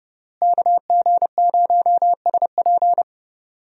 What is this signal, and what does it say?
Morse code, heard as a keyed tone, about 30 wpm, 700 Hz. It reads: KG0SP
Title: USA Call Signs - 5 Characters - Repeated 3x - 30wpm